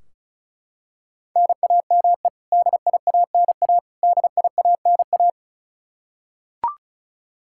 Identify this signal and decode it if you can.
Morse code, heard as a keyed tone, about 36 wpm, 700 Hz. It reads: NAME DIANA DIANA
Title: Practice QSO - Exchange 1 - Beginning - 35wpm